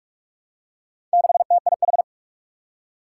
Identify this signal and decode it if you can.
Morse code, heard as a keyed tone, about 44 wpm, 700 Hz. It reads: 6TIH
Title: Four Challenging Letter-Number Combinations 45wpm